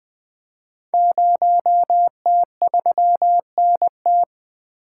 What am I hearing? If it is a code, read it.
Morse code, heard as a keyed tone, about 20 wpm, 700 Hz. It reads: 0T3NT
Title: Code Groups - 5 Letter:Numbers - 20wpm